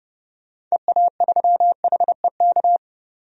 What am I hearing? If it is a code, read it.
Morse code, heard as a keyed tone, about 32 wpm, 700 Hz. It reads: EA3HEK
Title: DXCC-100 Call Signs - 6 Characters - 30wpm